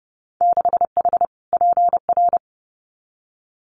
Morse code, heard as a keyed tone, about 32 wpm, 700 Hz. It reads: 6H PR
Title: Winter Field Day Exchanges Encore 30wpm